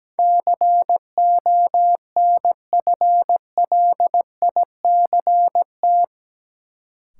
Morse code, heard as a keyed tone, about 17 wpm, 700 Hz. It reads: CONFLICT
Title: Sentences from Top 1300 Words - New Word Drill - 17wpm